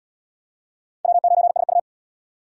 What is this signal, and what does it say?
Morse code, heard as a keyed tone, about 75 wpm, 700 Hz. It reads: W9SU